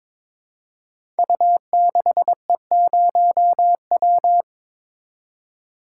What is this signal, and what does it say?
Morse code, heard as a keyed tone, about 23 wpm, 700 Hz. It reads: U6E0W
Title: Code Groups - 5 Letter:Numbers - 22wpm